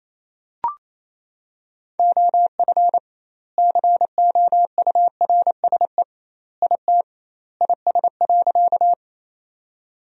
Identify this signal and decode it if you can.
Morse code, heard as a keyed tone, about 28 wpm, 700 Hz. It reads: OF COURSE IT IS.
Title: 2x Word-Spacing - Sentences from Top 500 Words - 28wpm